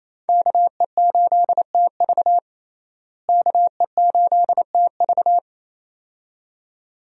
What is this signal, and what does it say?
Morse code, heard as a keyed tone, about 28 wpm, 700 Hz. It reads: KE8TV KE8TV
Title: USA Call Signs - 5 Characters - Repeated 2x - 28wpm